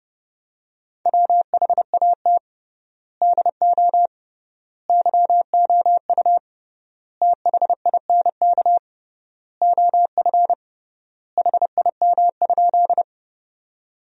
Morse code, heard as a keyed tone, about 31 wpm, 700 Hz. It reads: WHAT DO YOU THINK OF HIM?
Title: 3x Word-Spacing - Sentences from Top 200 Words - 30wpm